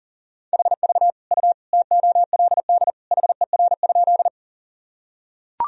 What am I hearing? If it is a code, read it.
Morse code, heard as a keyed tone, about 37 wpm, 700 Hz. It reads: HV U TOLD HER?
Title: Sentences from Top 400 Words and Abbreviations 40wpm